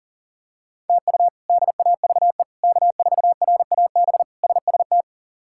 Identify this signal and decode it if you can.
Morse code, heard as a keyed tone, about 37 wpm, 700 Hz. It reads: TU DAVE K4RAB SST
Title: K1USN SST QSO Practice - 40wpm